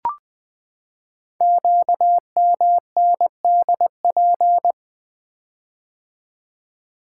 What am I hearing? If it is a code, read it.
Morse code, heard as a keyed tone, about 20 wpm, 700 Hz. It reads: QMNDP
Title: Code Groups - 5 Letters - 20wpm